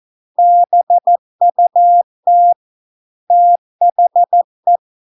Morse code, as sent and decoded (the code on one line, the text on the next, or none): -... ..- - / - .... .
BUT THE